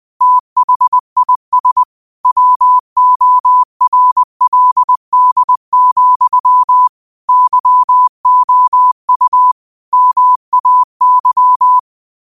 Morse code, heard as a keyed tone, about 20 wpm, 1000 Hz. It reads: THIS WORLD, YOU MAY
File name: qotdfiles / 20wpm